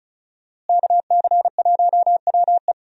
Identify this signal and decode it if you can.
Morse code, heard as a keyed tone, about 35 wpm, 700 Hz. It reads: KC1WE
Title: USA Call Signs - 5 Characters - Repeated 2x - 35wpm